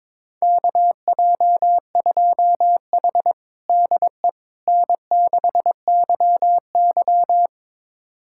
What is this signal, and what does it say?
Morse code, heard as a keyed tone, about 22 wpm, 700 Hz. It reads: KJ2H DE N6YY
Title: USA Callsign DE Callsign - 4 Characters - 22wpm